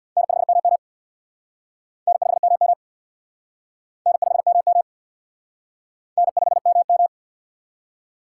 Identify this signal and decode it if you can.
Morse code, heard as a keyed tone, about 63 wpm, 700 Hz. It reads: N5MM N5MM N5MM N5MM